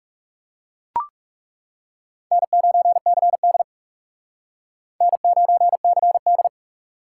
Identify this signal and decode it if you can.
Morse code, heard as a keyed tone, about 40 wpm, 700 Hz. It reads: N9CD N9CD